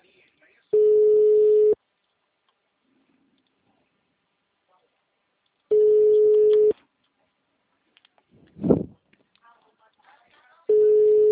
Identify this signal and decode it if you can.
Ringback tone (Europe, 425 Hz), with speech.
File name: out- -2002- - - .56957